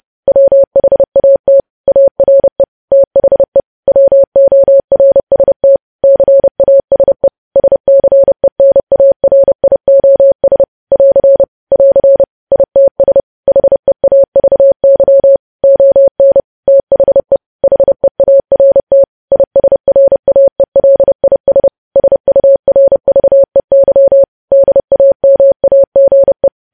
Morse code, heard as a keyed tone, about 30 wpm, 550 Hz. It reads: WHAT ARE THE WORST CASE SCENARIOS <AR> <AR> ITS HEAVY ON THE HEART ISRAELIS SURVEY DAMAGE
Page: News Headlines in Morse Code at 30 WPM